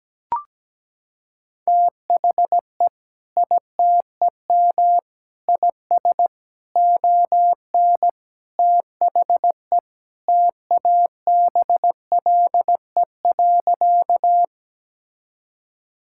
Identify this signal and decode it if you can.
Morse code, heard as a keyed tone, about 17 wpm, 700 Hz. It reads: THE ITEM IS ON THE TABLE.
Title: Sentences from Top 1000 Words - Review Part 6 - 17wpm